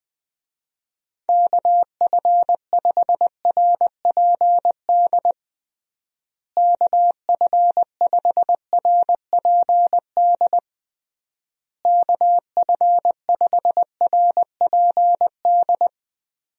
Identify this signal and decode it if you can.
Morse code, heard as a keyed tone, about 19 wpm, 700 Hz. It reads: KF5RPD KF5RPD KF5RPD
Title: USA Call Signs - 6 Characters - Repeated 3x - 20wpm